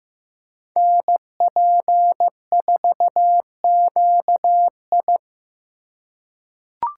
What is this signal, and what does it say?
Morse code, heard as a keyed tone, about 15 wpm, 700 Hz. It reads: NP4QI
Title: USA Call Signs - 5 Characters - 3x Character Spacing - 15wpm